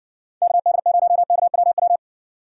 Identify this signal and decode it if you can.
Morse code, heard as a keyed tone, about 58 wpm, 700 Hz. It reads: KD0KWW